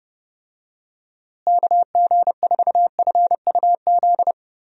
Morse code, heard as a keyed tone, about 30 wpm, 700 Hz. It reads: KG4FUZ
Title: USA Call Signs - 6 Characters - Repeated 2x - 30wpm